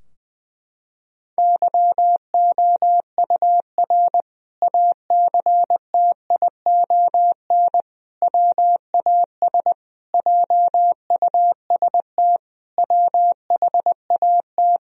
Morse code, heard as a keyed tone, about 20 wpm, 700 Hz. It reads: YOUR ACTION WAS JUST WHAT
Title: Sentences from Top 1000 Words - Review Part 4 - 20wpm